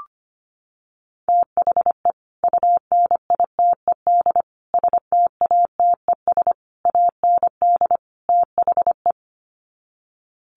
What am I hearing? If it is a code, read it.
Morse code, heard as a keyed tone, about 25 wpm, 700 Hz. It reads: THE UNITED STATES AND THE
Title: Sets of 5 Words 25wpm